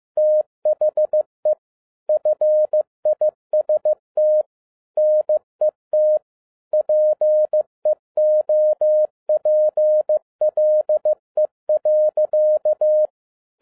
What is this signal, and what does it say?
Morse code, heard as a keyed tone, about 15 wpm, 610 Hz. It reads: THE FIST NET PEOPLE.